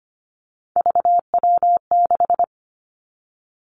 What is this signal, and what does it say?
Morse code, heard as a keyed tone, about 25 wpm, 700 Hz. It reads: VW6
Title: Three Challenging Letter-Number Combinations 25wpm